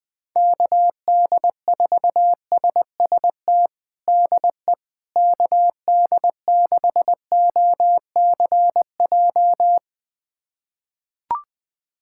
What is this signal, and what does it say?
Morse code, heard as a keyed tone, about 21 wpm, 700 Hz. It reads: KD4SST DE KD6OCJ
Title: USA Callsign DE Callsign - 6 Characters - 20wpm